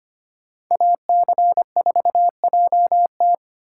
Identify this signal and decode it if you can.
Morse code, heard as a keyed tone, about 33 wpm, 700 Hz. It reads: AC4JT